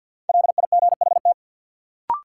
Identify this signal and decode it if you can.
Morse code, heard as a keyed tone, about 52 wpm, 700 Hz. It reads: LIGHT
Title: Sentences from Top 200 Words - New Word Drill - 50wpm